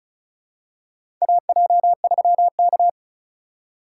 Morse code, heard as a keyed tone, about 36 wpm, 700 Hz. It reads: AJ3K
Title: USA Call Signs - 4 Characters - 6x Character Spacing - 35wpm